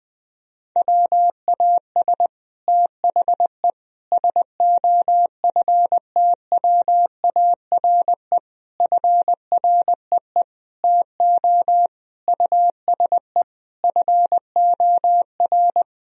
Morse code, heard as a keyed tone, about 20 wpm, 700 Hz. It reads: WAS THE SOFTWARE FREE TO USE FOR
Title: Sentences from Top 1000 Words - Review Part 3 - 20wpm